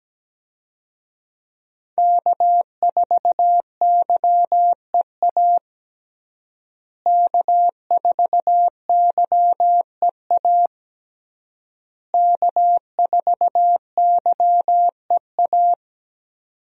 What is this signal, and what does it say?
Morse code, heard as a keyed tone, about 17 wpm, 700 Hz. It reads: K4YEA K4YEA K4YEA
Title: USA Call Signs - 5 Characters - Repeated 3x - 17wpm